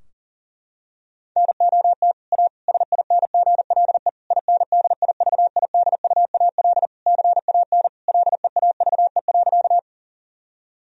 Morse code, heard as a keyed tone, about 40 wpm, 700 Hz. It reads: NOT A SINGLE INDIVIDUAL CAN LEAVE.
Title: Sentences from Top 1000 Words - Review Part 4 - 40wpm